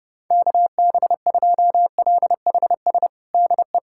Morse code, heard as a keyed tone, about 32 wpm, 700 Hz. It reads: KB2LHS DE
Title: USA Callsign DE Callsign 30wpm